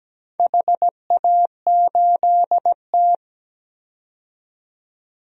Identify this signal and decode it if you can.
Morse code, heard as a keyed tone, about 17 wpm, 700 Hz. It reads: HA8T
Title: DXCC-100 Call Signs - 4 Characters - 17wpm